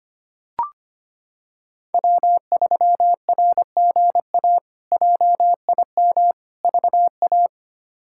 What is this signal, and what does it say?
Morse code, heard as a keyed tone, about 25 wpm, 700 Hz. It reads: W3RGA JIM VA
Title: K1USN SST QSO Practice - 25wpm